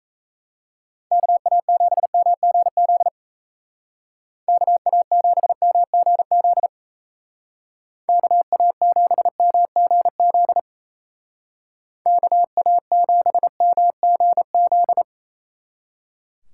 Morse code, heard as a keyed tone, about 37 wpm, 700 Hz. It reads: KA7MGZ KA7MGZ KA7MGZ KA7MGZ